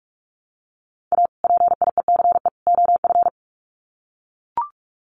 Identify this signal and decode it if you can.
Morse code, heard as a keyed tone, about 44 wpm, 700 Hz. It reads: A PIECE OF
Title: Sets of 3 Words 45wpm